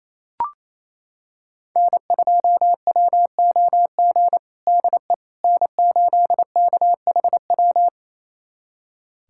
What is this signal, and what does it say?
Morse code, heard as a keyed tone, about 28 wpm, 700 Hz. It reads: N2WOG DE N8KHW
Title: USA Callsign DE Callsign - 5 Characters - 28wpm